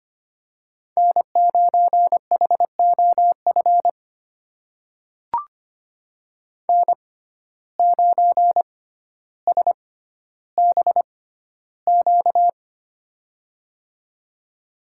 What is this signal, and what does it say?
Morse code, heard as a keyed tone, about 25 wpm, spaced as for 13 wpm, 700 Hz. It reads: N9HOF N9SBQ
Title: USA Call Signs - 5 Characters - 6x Character Spacing - 25wpm